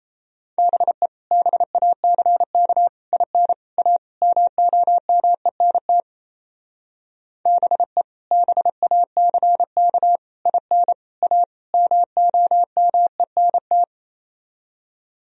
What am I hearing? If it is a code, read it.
Morse code, heard as a keyed tone, about 33 wpm, 700 Hz. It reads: BE BACK IN A MOMENT BE BACK IN A MOMENT
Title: Speed-Racing - Sets of 5 Words 28wpm